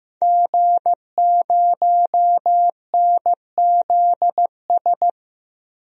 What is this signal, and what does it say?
Morse code, heard as a keyed tone, about 16 wpm, 700 Hz. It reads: G0NZS
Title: DXCC-100 Call Signs - 5 Characters - 15wpm